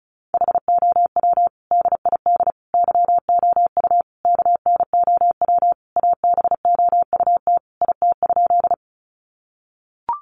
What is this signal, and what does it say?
Morse code, heard as a keyed tone, about 34 wpm, 700 Hz. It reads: HOW DID YOU KNOW ABOUT IT?
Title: Sentences from Top 100 Words 35wpm